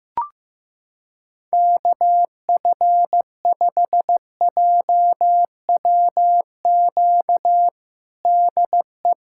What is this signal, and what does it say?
Morse code, heard as a keyed tone, about 15 wpm, 700 Hz. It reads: KF5JWQ DE
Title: USA Callsign DE Callsign - 6 Characters - 15wpm